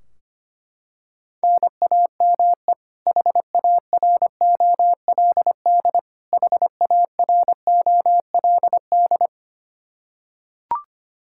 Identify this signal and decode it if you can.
Morse code, heard as a keyed tone, about 25 wpm, 700 Hz. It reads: NAME HAROLD HAROLD
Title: Practice QSO - Exchange 1 - Beginning - 25wpm